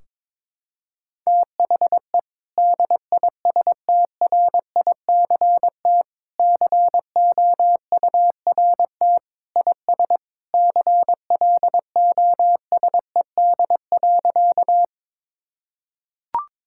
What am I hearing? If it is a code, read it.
Morse code, heard as a keyed tone, about 22 wpm, 700 Hz. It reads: THE DISTRICT COURT IS CLOSED.
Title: Sentences from Top 1600 Words 22wpm